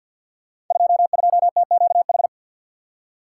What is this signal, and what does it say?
Morse code, heard as a keyed tone, about 47 wpm, 700 Hz. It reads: 22TQH
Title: Code Groups - 5 Letter:Numbers - 50wpm